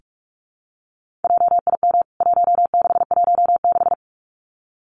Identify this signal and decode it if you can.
Morse code, heard as a keyed tone, about 44 wpm, 700 Hz. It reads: JIM 1616
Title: CWT Exchange 45wpm